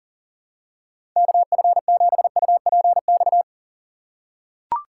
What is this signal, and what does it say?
Morse code, heard as a keyed tone, about 40 wpm, 700 Hz. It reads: KF7UPX
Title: USA Call Signs - 6 Characters - 4x Character Spacing - 40wpm